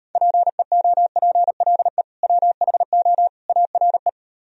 Morse code, heard as a keyed tone, about 38 wpm, 700 Hz. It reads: PEOPLE WHO ARE